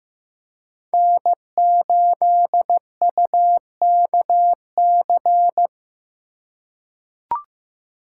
Morse code, heard as a keyed tone, about 15 wpm, 700 Hz. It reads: N8UKC
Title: USA Call Signs - 5 Characters - 3x Character Spacing - 15wpm